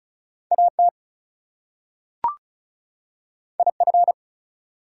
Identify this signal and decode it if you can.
Morse code, heard as a keyed tone, about 34 wpm, 700 Hz. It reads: AT IF
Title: Two Letter Words - 35wpm